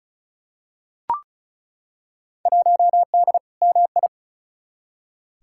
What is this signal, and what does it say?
Morse code, heard as a keyed tone, about 36 wpm, 700 Hz. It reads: 1D MI
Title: Field Day Exchanges - 35wpm